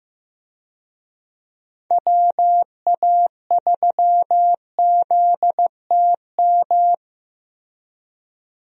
Morse code, heard as a keyed tone, about 15 wpm, 700 Hz. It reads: WA3ZTM
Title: USA Call Signs - 6 Characters - Repeated 2x - 15wpm